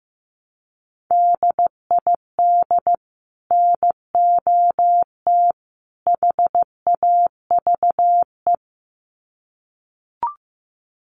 Morse code, heard as a keyed tone, about 15 wpm, 700 Hz. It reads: DID NOT HAVE
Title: Sets of 3 Words 15wpm